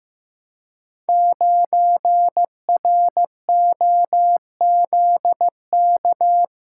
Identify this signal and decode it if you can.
Morse code, heard as a keyed tone, about 15 wpm, 700 Hz. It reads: 9ROZK
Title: Code Groups - 5 Letter:Numbers - 15wpm